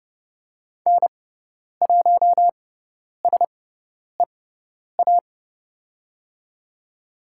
Morse code, heard as a keyed tone, about 30 wpm, spaced as for 10 wpm, 700 Hz. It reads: N1SEA
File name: Lesson 17g - Letter u - Call Signs 30wpm